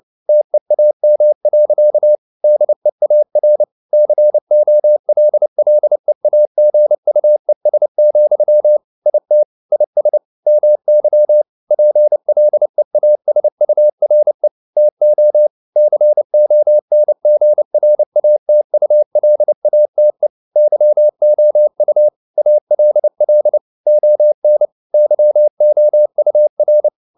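Morse code, heard as a keyed tone, about 29 wpm, 600 Hz. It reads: TEAM. DEAR COLLEAGUES, IT IS MY PLEASURE TO CONGRATULATE YOU ALL ON YOUR
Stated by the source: Morse code at 29 wpm, sending Ken Follett's novel Never in English, from CW med Gnister.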